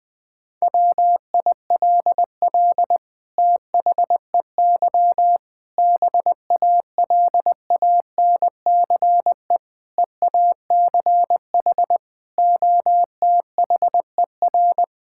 Morse code, heard as a keyed tone, about 20 wpm, 700 Hz. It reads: WILL THEY BALANCE EACH OTHER
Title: Sentences from Top 1000 Words - Review Part 6 - 20wpm